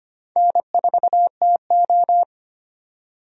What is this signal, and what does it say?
Morse code, heard as a keyed tone, about 25 wpm, 700 Hz. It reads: N4TO
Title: USA Call Signs - 4 Characters - 25wpm